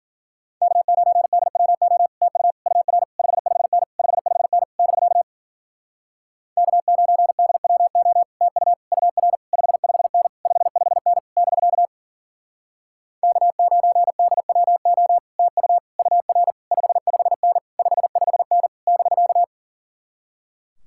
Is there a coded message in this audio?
Yes: Morse code from 0.6 to 19.4 s, heard as a keyed tone, about 53 wpm, 700 Hz. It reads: K9DWO TU UR 55N 55N <BK> K9DWO TU UR 55N 55N <BK> K9DWO TU UR 55N 55N <BK>